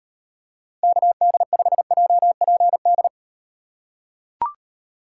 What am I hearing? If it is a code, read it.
Morse code, heard as a keyed tone, about 38 wpm, 700 Hz. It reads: KD5JPD